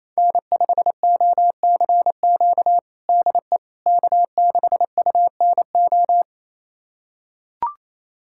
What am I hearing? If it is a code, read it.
Morse code, heard as a keyed tone, about 28 wpm, 700 Hz. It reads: N5OCQ DE K6UNO
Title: USA Callsign DE Callsign - 5 Characters - 28wpm